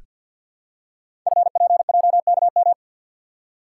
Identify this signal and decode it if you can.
Morse code, heard as a keyed tone, about 47 wpm, 700 Hz. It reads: FPJKM
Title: Code Groups - 5 Letter:Numbers - 50wpm